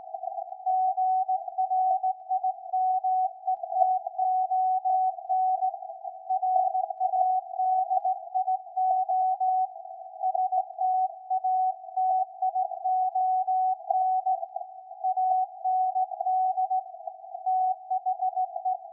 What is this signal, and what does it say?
Morse code, heard as a keyed tone, about 17 wpm, 750 Hz. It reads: GRIMETON RADIO STATION AND THE
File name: iz7slz / VLF / SAQ 13-02-2023